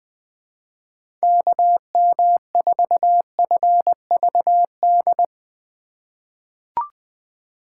Morse code, heard as a keyed tone, about 20 wpm, 700 Hz. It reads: KM4FVD
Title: USA Call Signs - 6 Characters - 2x Character Spacing - 20wpm